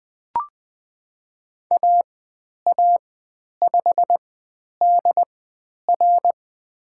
Morse code, heard as a keyed tone, about 20 wpm, spaced as for 10 wpm, 700 Hz. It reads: AA5DR
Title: 109 Lesson 18g - Letter m - Call Signs 20wpm